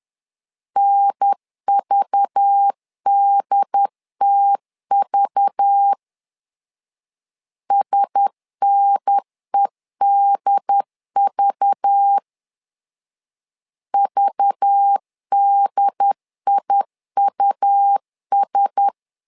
Morse code, heard as a keyed tone, about 10 wpm, 800 Hz. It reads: NVDTV SNEDV VDIUS
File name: LEZIONI / 5 LBVC / 5 LBV 10wpm 2248